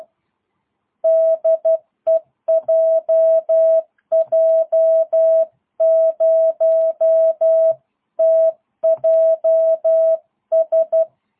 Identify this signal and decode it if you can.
Morse code, heard as a keyed tone, about 12 wpm, 660 Hz. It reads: DE JJ0TJS
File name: jh8cbh / 2022cw